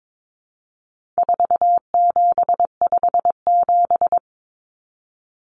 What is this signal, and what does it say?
Morse code, heard as a keyed tone, about 22 wpm, 700 Hz. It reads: 4757
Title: Numbers - 4-Digit 22wpm